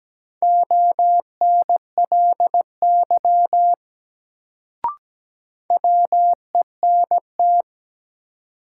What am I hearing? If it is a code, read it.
Morse code, heard as a keyed tone, about 17 wpm, 700 Hz. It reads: ONLY WENT
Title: Sets of 2 Words - New Word Drill - 17wpm